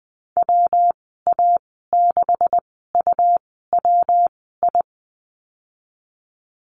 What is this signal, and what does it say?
Morse code, heard as a keyed tone, about 20 wpm, spaced as for 15 wpm, 700 Hz. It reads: WA6UWI
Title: 128 Lesson 21g - Number 6 - Call Signs 20wpm